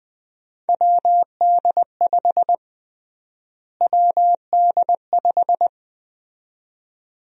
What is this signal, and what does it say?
Morse code, heard as a keyed tone, about 20 wpm, 700 Hz. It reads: WD5 WD5
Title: USA Call Sign Prefixes - Repeated 2x - 20wpm